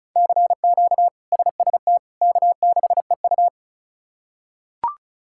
Morse code, heard as a keyed tone, about 36 wpm, 700 Hz. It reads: CQ SST K6EU
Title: K1USN SST QSO Practice - 35wpm